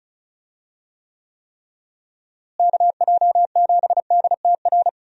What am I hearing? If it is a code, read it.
Morse code, heard as a keyed tone, about 36 wpm, 700 Hz. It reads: KJ7DTR
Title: USA Call Signs - 6 Characters - Repeated 3x - 35wpm